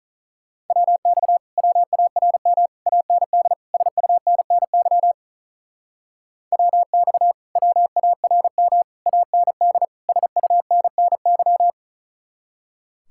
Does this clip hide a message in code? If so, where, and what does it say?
Yes: Morse code from 0.7 to 11.7 s, heard as a keyed tone, about 44 wpm, 700 Hz. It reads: WX WARM AND SUNNY WX WARM AND SUNNY